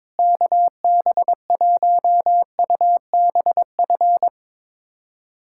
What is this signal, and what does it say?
Morse code, heard as a keyed tone, about 22 wpm, 700 Hz. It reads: KB1UBF
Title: USA Callsign DE Callsign - 6 Characters - 22wpm